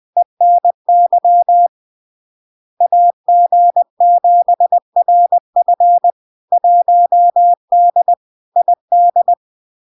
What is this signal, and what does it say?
Morse code, heard as a keyed tone, about 20 wpm, 700 Hz. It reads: ENY AG7RF 1D ID
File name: LARK-FD-2022-Log-in-CW-20wpm-700-Hz